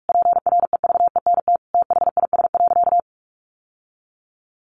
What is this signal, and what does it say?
Morse code, heard as a keyed tone, about 46 wpm, 700 Hz. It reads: PREVENT THIS.
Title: Sentences from Top 1100 Words 45wpm